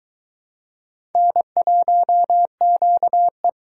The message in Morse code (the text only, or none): N1QE